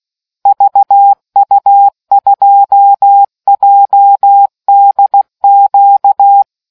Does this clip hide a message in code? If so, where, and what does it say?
Yes: Morse code from 0.5 to 6.4 s, heard as a keyed tone, about 16 wpm, 800 Hz. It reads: VU2JDQ